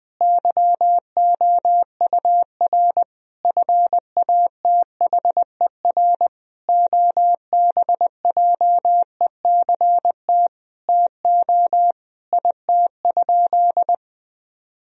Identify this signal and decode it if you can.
Morse code, heard as a keyed tone, about 20 wpm, 700 Hz. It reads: YOUR FATHER OBJECT TO IT?
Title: Sentences from Top 1000 Words - Review Part 3 - 20wpm